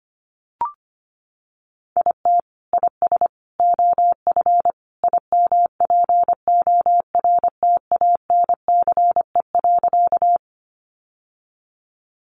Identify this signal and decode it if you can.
Morse code, heard as a keyed tone, about 25 wpm, 700 Hz. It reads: IT IS OF IMPORTANCE.
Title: Sentences from Top 1200 Words 25wpm